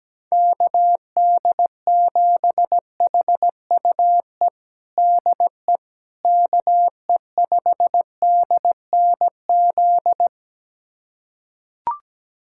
Morse code, heard as a keyed tone, about 17 wpm, 700 Hz. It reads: KD7HUE DE KE5DNZ
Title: USA Callsign DE Callsign - 6 Characters - 17wpm